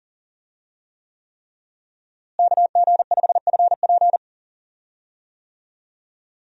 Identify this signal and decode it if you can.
Morse code, heard as a keyed tone, about 38 wpm, 700 Hz. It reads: KG5FP
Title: USA Call Signs - 5 Characters - Repeated 3x - 40wpm